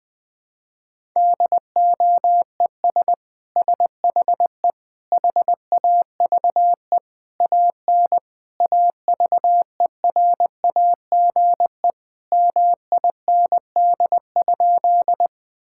Morse code, heard as a keyed tone, about 21 wpm, 700 Hz. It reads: DOES SHE HAVE AN AVERAGE MIND?
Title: Sentences from Top 1000 Words - Review Part 6 - 20wpm